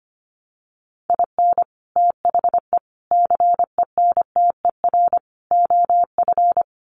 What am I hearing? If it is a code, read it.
Morse code, heard as a keyed tone, about 25 wpm, 700 Hz. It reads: IN THE CENTER OF
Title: Sets of 4 Words 25wpm